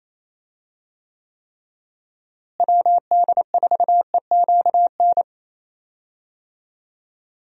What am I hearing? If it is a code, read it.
Morse code, heard as a keyed tone, about 28 wpm, 700 Hz. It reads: WD4EQN
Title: USA Call Signs - 6 Characters - Repeated 3x - 28wpm